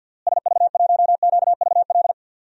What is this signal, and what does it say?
Morse code, heard as a keyed tone, about 51 wpm, 700 Hz. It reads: SV1QVL
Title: DXCC-100 Call Signs - 6 Characters - Repeated 2x - 50wpm